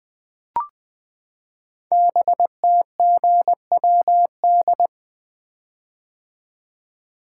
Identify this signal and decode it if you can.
Morse code, heard as a keyed tone, about 20 wpm, 700 Hz. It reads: BTGWD
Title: Code Groups - 5 Letters - 20wpm